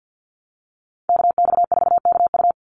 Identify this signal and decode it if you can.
Morse code, heard as a keyed tone, about 50 wpm, 700 Hz. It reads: KX4KU
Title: Call Signs - 50wpm